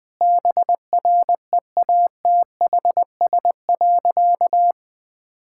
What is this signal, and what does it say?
Morse code, heard as a keyed tone, about 21 wpm, 700 Hz. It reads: BREATHS.
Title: Easy Sentences with Hints - 20wpm